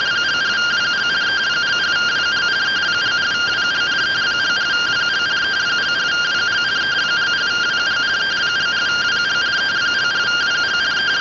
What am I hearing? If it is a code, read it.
Two-tone teleprinter signal, about 170 Hz shift.